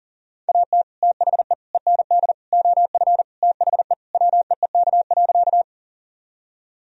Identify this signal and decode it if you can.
Morse code, heard as a keyed tone, about 40 wpm, 700 Hz. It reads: AT THE END OF THE WEEK.
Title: Sentences from Top 1000 Words - Review Part 2 - 40wpm